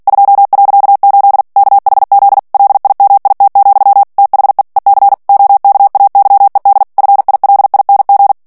Morse code, heard as a keyed tone, about 48 wpm, 800 Hz. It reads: 2.8 KHZ LIMIT, THE EC OKAYED FILING
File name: ANGLAIS texte clair 3 vitesse 45 mots minute